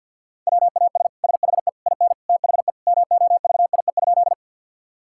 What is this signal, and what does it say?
Morse code, heard as a keyed tone, about 50 wpm, 700 Hz. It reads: WAS SHE IN THE MOVIE?
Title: Sentences from Top 1000 Words - Review Part 5 - 50wpm